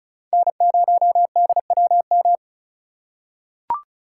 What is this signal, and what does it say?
Morse code, heard as a keyed tone, about 35 wpm, 690 Hz. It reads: N0DWM
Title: USA Call Signs - 5 Characters - 2x Character Spacing - 35wpm